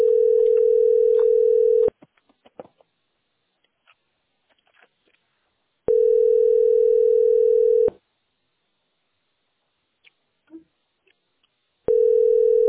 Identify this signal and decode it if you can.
Ringback tone (North America), 440 + 480 Hz.